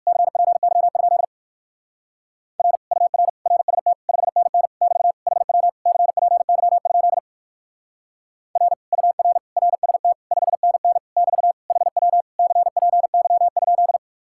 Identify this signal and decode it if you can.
Morse code, heard as a keyed tone, about 57 wpm, 700 Hz. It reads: CPY? R UR RST 5NN <BT> HW CPY? R UR RST 5NN <BT> HW CPY?